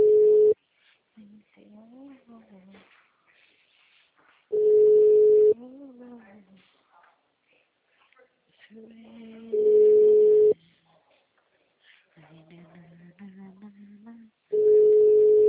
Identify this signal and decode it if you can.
Ringback tone (Europe, 425 Hz), with speech.
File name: out- -2006- - - .50030